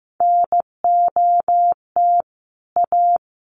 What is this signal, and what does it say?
Morse code, heard as a keyed tone, about 15 wpm, 700 Hz. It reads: NOT A
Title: Sentences from Top 1400 Words 15wpm